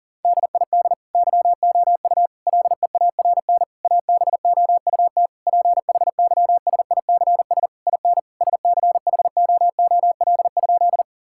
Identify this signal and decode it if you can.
Morse code, heard as a keyed tone, about 40 wpm, 700 Hz. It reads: DID YOU LEARN ABOUT PHYSICS IN SCHOOL?
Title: Sentences from Top 1000 Words - Review Part 5 - 40wpm